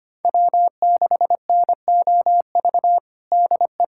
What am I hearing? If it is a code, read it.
Morse code, heard as a keyed tone, about 25 wpm, 700 Hz. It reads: W6NOV DE
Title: USA Callsign DE Callsign - 5 Characters - 25wpm